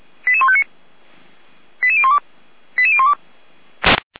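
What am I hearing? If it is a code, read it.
Five-tone selective call (ZVEI), heard as quick runs of five tones, tones of about 70 ms. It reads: 88159 88012 88012
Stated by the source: ZVEI five-tone call 88149 received by SDR, feuerwehr-orlamuende.org fire brigade audio server.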